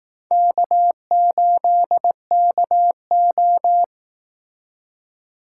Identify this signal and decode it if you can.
Morse code, heard as a keyed tone, about 18 wpm, 700 Hz. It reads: K8KO
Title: Speed Racing - USA 1x2 Call Signs 15wpm